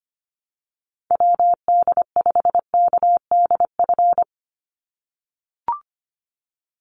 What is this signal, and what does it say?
Morse code, heard as a keyed tone, about 25 wpm, 700 Hz. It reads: WD5KDF
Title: USA Call Signs 25wpm